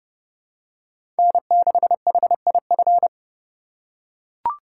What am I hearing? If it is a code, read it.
Morse code, heard as a keyed tone, about 30 wpm, 700 Hz. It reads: N6HIF
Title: USA Call Signs - 5 Characters - 2x Character Spacing - 30wpm